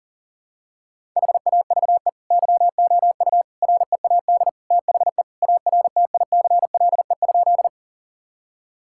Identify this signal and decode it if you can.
Morse code, heard as a keyed tone, about 37 wpm, 700 Hz. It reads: HAVE YOU READ THE ARTICLE?
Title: Sentences from Top 1000 Words - Review Part 5 - 40wpm